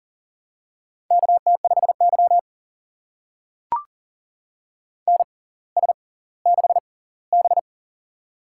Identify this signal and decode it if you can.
Morse code, heard as a keyed tone, about 38 wpm, spaced as for 20 wpm, 700 Hz. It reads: KT5Y NS6B
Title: USA Call Signs - 4 Characters - 6x Character Spacing - 40wpm